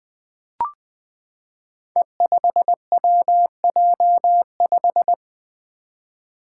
Morse code, heard as a keyed tone, about 20 wpm, 700 Hz. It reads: E5WJ5
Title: Five Challenging Letter-Number Combinations 20wpm